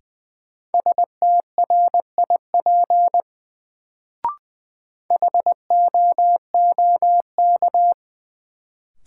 Morse code, heard as a keyed tone, about 20 wpm, 700 Hz. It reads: STRIP HOOK
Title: Sentences from Top 1600 Words - New Word Drill Alt - 20wpm